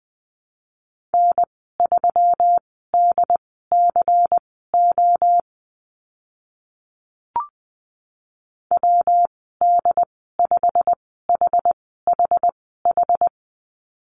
Morse code, heard as a keyed tone, about 20 wpm, spaced as for 14 wpm, 700 Hz. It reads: N3DCO WD5HHH
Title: 122 Lesson 20g - Number 3 - Call Signs 20wpm